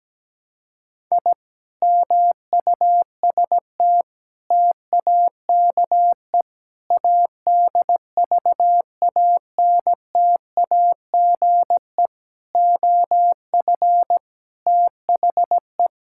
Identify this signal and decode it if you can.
Morse code, heard as a keyed tone, about 17 wpm, 700 Hz. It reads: I MUST TAKE ADVANTAGE OF THE
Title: Sentences from Top 1000 Words - Review Part 6 - 17wpm